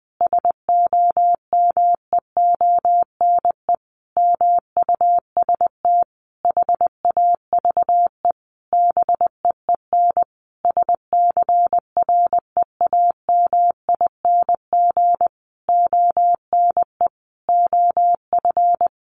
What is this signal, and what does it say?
Morse code, heard as a keyed tone, about 20 wpm, 700 Hz. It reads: SOMEONE MUST HAVE BEEN SCREAMING ONE OF
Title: A Sound Of Thunder 20wpm